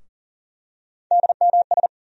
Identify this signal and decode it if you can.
Morse code, heard as a keyed tone, about 40 wpm, 700 Hz. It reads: DMS
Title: USA Call Sign Postfixes - 3 Letters 40wpm